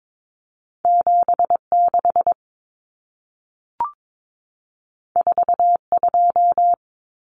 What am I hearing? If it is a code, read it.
Morse code, heard as a keyed tone, about 22 wpm, 700 Hz. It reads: 76 42
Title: Numbers - 2-Digit 22wpm